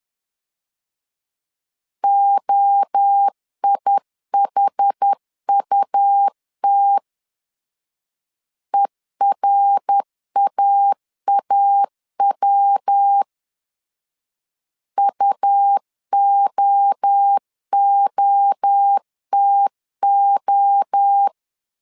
Morse code, heard as a keyed tone, about 10 wpm, 800 Hz. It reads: OIHUT ERAAW UOOTO
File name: CW / DET / 1 3 1500C S B 10wpm 3335